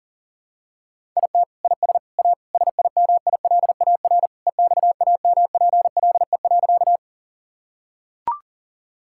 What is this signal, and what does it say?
Morse code, heard as a keyed tone, about 37 wpm, 700 Hz. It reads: IT IS A SIMILAR EXAMPLE.
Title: Sentences from Top 1000 Words - Review Part 5 - 40wpm